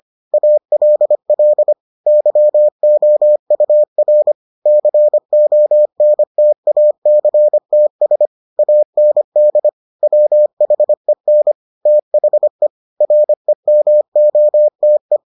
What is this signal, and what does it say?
Morse code, heard as a keyed tone, about 25 wpm, 600 Hz. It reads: ALL YOUR CONTACTS AND WHEN THE REMOTE